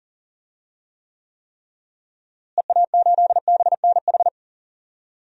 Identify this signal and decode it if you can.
Morse code, heard as a keyed tone, about 40 wpm, 700 Hz. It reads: EA8BNH
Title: DXCC-100 Call Signs - 6 Characters - Repeated 3x - 40wpm